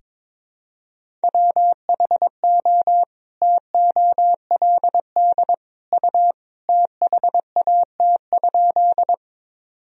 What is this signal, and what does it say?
Morse code, heard as a keyed tone, about 22 wpm, 700 Hz. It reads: WHO TOLD U THAT?
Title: Sentences from Top 400 Words and Abbreviations 22wpm